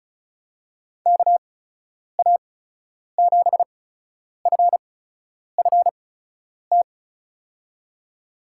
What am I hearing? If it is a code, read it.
Morse code, heard as a keyed tone, about 36 wpm, spaced as for 10 wpm, 700 Hz. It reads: KA7FFT
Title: USA Call Signs - 6 Characters - 8x Character Spacing - 35wpm